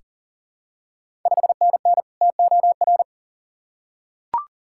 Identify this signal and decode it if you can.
Morse code, heard as a keyed tone, about 40 wpm, 700 Hz. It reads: 5NN TOR
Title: Ontario QSO Party Exchange 40wpm